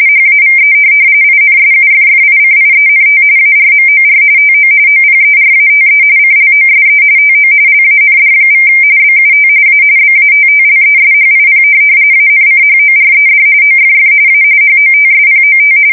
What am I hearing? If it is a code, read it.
RTTY, about 45.45 baud. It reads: HAVE INSUFFICIENT ARMOR TO OPERATE CLOSE TO LAND= BUT A BATTLESHIP ANCHORED OFF A HOTSPOT
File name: fsk170 / USS-New-Jersey